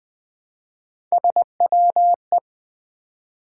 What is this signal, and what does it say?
Morse code, heard as a keyed tone, about 20 wpm, 700 Hz. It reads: SWE
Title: Build A Word - 1 Letter At A Time - 5-Letter Words - 20wpm